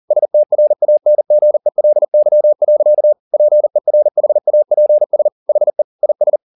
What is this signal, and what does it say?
Morse code, heard as a keyed tone, about 40 wpm, 600 Hz. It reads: STRANGELY. PERHAPS HE IS